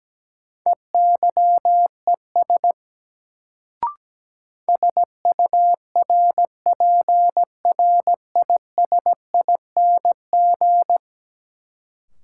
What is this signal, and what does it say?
Morse code, heard as a keyed tone, about 17 wpm, 700 Hz. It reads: EYES SURPRISING
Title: Sets of 5 Words - New Word Drill - 17wpm